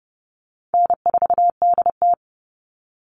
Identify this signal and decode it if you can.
Morse code, heard as a keyed tone, about 30 wpm, 700 Hz. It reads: N4DT
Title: Call Signs Repeated 2x - 30wpm